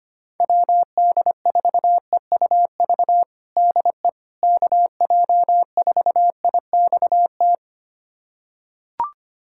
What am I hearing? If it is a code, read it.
Morse code, heard as a keyed tone, about 25 wpm, 700 Hz. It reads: WD4EUV DE KJ4IXT
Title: USA Callsign DE Callsign - 6 Characters - 25wpm